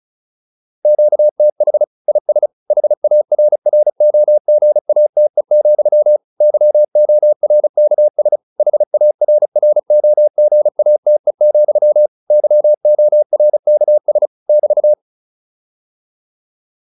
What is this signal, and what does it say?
Morse code, heard as a keyed tone, about 34 wpm, 600 Hz. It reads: QTH IS HARROGATE, YORKS HARROGATE, YORKS <BT>